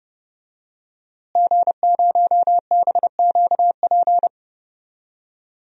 Morse code, heard as a keyed tone, about 31 wpm, 700 Hz. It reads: G0BQP
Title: DXCC-100 Call Signs - 5 Characters - Repeated 3x - 30wpm